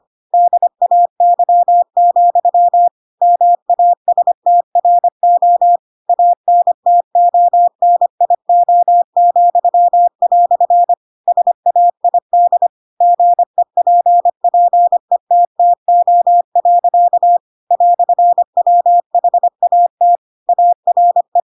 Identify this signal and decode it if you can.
Morse code, heard as a keyed tone, about 25 wpm, 700 Hz. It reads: DAY, MASTRO ANTONIO," SAID GEPPETTO. "WHAT ARE
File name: Pinocchio / Pinocchio ch2 25 25 ik1hge